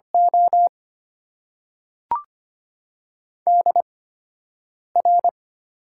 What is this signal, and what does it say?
Morse code, heard as a keyed tone, about 25 wpm, spaced as for 7 wpm, 700 Hz. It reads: O DR
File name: Top 90 QSO Elements - 8x Character Spacing - 25wpm